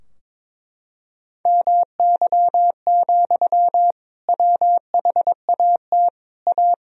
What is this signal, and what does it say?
Morse code, heard as a keyed tone, about 22 wpm, 700 Hz. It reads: MY, WHAT A
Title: Sentences from Top 1000 Words - Review Part 2 - 22wpm